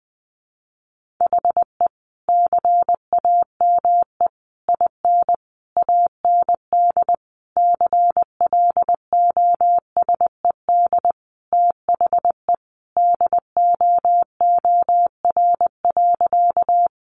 Morse code, heard as a keyed tone, about 20 wpm, 700 Hz. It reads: HE CAME IN AND CLOSED THE DOOR.
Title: Sentences from Top 1400 Words 20wpm